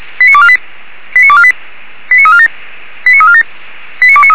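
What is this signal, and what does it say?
Five-tone selective call (ZVEI), heard as quick runs of five tones, tones of about 70 ms. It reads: 89248 89248 89347 89347 89293
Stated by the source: ZVEI five-tone selective call received on SDR.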